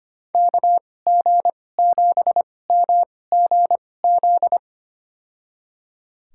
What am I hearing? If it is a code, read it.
Morse code, heard as a keyed tone, about 25 wpm, spaced as for 18 wpm, 700 Hz. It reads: KG7MGZ
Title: USA Call Signs - 2x Character Spacing - 25wpm